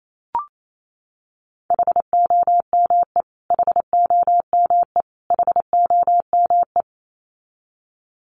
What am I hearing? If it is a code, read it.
Morse code, heard as a keyed tone, about 28 wpm, 700 Hz. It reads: HOME HOME HOME
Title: Top 200 Words Repeated 28wpm